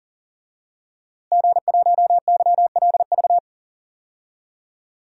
Morse code, heard as a keyed tone, about 40 wpm, 700 Hz. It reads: G1YLV
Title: DXCC-100 Call Signs - 5 Characters - Repeated 3x - 40wpm